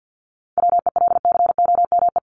Morse code, heard as a keyed tone, about 48 wpm, 700 Hz. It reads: WELCOME
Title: Top 1751-2000 Words 50wpm